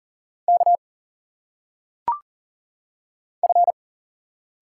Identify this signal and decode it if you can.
Morse code, heard as a keyed tone, about 38 wpm, 700 Hz. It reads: K F
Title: Single Letters - Phonetic Pronunciation 40wpm